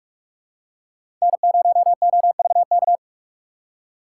Morse code, heard as a keyed tone, about 46 wpm, 700 Hz. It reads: N0OVK
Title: USA Call Signs - 5 Characters - 3x Character Spacing - 45wpm